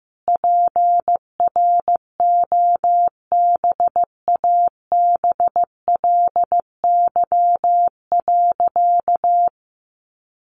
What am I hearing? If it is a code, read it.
Morse code, heard as a keyed tone, about 15 wpm, 700 Hz. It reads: PROBABLY.
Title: Sentences from Top 700 Words 15wpm